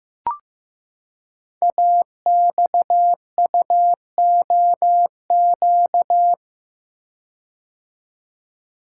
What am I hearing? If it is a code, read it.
Morse code, heard as a keyed tone, about 15 wpm, 700 Hz. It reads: AXUOQ
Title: Code Groups - 5 Letters - 15wpm